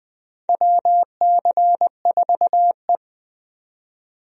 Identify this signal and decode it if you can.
Morse code, heard as a keyed tone, about 20 wpm, 700 Hz. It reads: WC4E
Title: USA Call Signs - 4 Characters - 20wpm